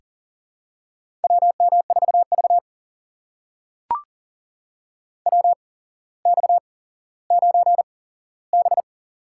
Morse code, heard as a keyed tone, about 40 wpm, spaced as for 17 wpm, 700 Hz. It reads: WM4V WX9B
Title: USA Call Signs - 4 Characters - 8x Character Spacing - 40wpm